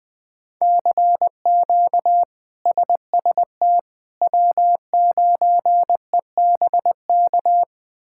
Morse code, heard as a keyed tone, about 20 wpm, 700 Hz. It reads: CQ SST W9EBK
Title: K1USN SST QSO Practice - 20wpm